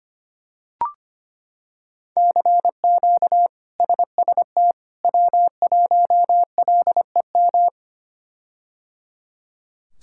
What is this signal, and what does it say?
Morse code, heard as a keyed tone, about 25 wpm, 700 Hz. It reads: CQ SST W1LEM
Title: K1USN SST QSO Practice - 25wpm